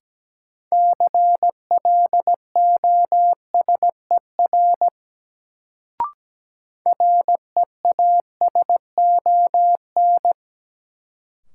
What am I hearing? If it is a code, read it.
Morse code, heard as a keyed tone, about 17 wpm, 700 Hz. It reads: CLOSER REASON
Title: Sets of 5 Words Encore - New Word Drill - 17wpm